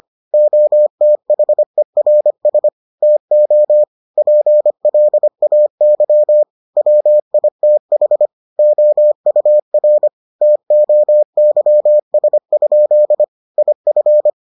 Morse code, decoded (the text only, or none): OTHERS TO PLAY WITH OUR TOYS? IF